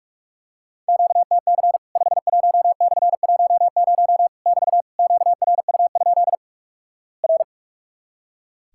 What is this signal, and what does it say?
Morse code, heard as a keyed tone, about 46 wpm, 700 Hz. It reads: QTC 51/10 <BT> QRU? R
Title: Worked All Europe DX Contest - QTC Exchange - 45wpm